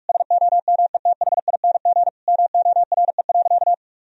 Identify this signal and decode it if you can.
Morse code, heard as a keyed tone, about 46 wpm, 700 Hz. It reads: SOMETHING MORE.
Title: Sentences from Top 1000 Words - Review Part 1 - 45wpm